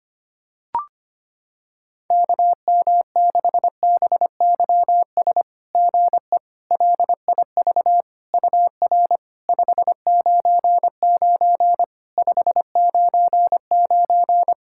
Morse code, heard as a keyed tone, about 26 wpm, 700 Hz. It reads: KM6BYS GE LIV UR 599 599
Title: SOTA QSO Practice 25wpm